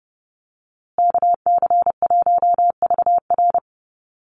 Morse code, heard as a keyed tone, about 30 wpm, 700 Hz. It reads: KC1VR
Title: USA Call Signs Repeated 3x 30wpm